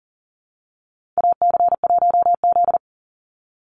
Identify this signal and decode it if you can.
Morse code, heard as a keyed tone, about 40 wpm, 700 Hz. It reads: AC1Z
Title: NA SOTA Activator Callsigns 40wpm